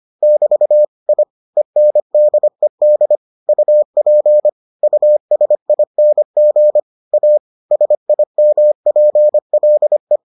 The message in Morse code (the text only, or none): <BT> I ENDED UP USING A SIMPLE